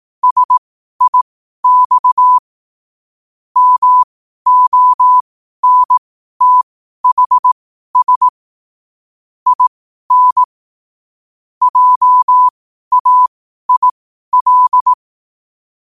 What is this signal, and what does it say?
Morse code, heard as a keyed tone, about 18 wpm, spaced as for 12 wpm, 1000 Hz. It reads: SIX MONTHS IN JAIL
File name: qotdfiles / 12wpm